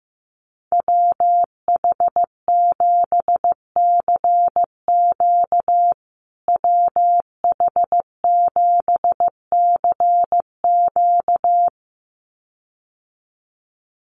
Morse code, heard as a keyed tone, about 15 wpm, 700 Hz. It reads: WH7CQ WH7CQ
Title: USA Call Signs Repeated 2x 15wpm